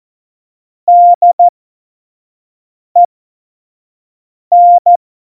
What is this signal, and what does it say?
Morse code, heard as a keyed tone, about 14 wpm, spaced as for 5 wpm, 700 Hz. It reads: DEN
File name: Morsekurs / Witze / Witz 27. Oktober 2020, 25 F70